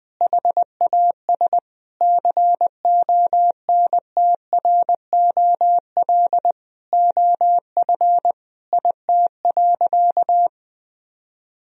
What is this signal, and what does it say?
Morse code, heard as a keyed tone, about 21 wpm, 700 Hz. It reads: HAS CONTROL OF IT.
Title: Sentences from Top 1000 Words - Review Part 3 - 20wpm